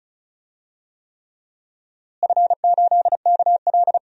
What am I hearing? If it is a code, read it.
Morse code, heard as a keyed tone, about 34 wpm, 700 Hz. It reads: F8KL
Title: DXCC-100 Call Signs - 4 Characters - Repeated 3x - 35wpm